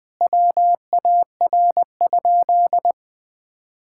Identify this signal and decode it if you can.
Morse code, heard as a keyed tone, about 20 wpm, 700 Hz. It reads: WAR?
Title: Sentences from Top 1600 Words 20wpm